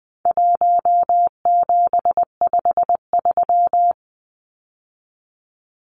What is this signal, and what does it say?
Morse code, heard as a keyed tone, about 20 wpm, 700 Hz. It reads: 1753
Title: Numbers - 4-Digit 20wpm